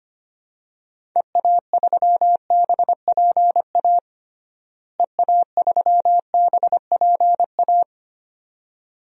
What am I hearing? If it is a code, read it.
Morse code, heard as a keyed tone, about 25 wpm, 700 Hz. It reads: EA3BPA EA3BPA
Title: DXCC-100 Call Signs - 6 Characters - Repeated 2x - 25wpm